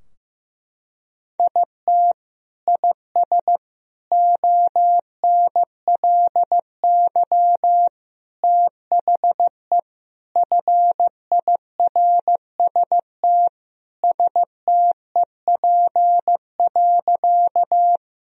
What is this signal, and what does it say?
Morse code, heard as a keyed tone, about 15 wpm, 700 Hz. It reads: IT IS ONLY THE FIRST STEP.
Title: Sentences from Top 1000 Words - Review Part 2 - 15wpm